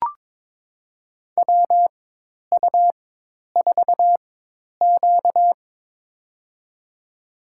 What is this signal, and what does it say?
Morse code, heard as a keyed tone, about 22 wpm, spaced as for 10 wpm, 700 Hz. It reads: WU4Q
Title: USA Call Signs - 4 Characters - 4x Character Spacing - 22wpm